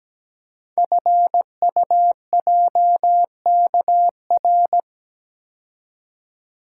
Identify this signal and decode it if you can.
Morse code, heard as a keyed tone, about 17 wpm, 700 Hz. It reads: FUJKR
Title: Code Groups - 5 Letters - 17wpm